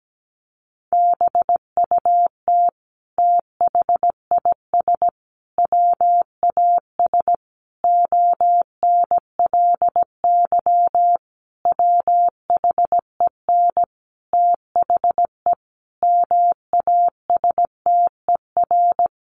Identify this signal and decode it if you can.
Morse code, heard as a keyed tone, about 17 wpm, 700 Hz. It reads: BUT THIS WAS ONLY WHEN THE MASTER
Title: Sentences from Top 1300 Words 17wpm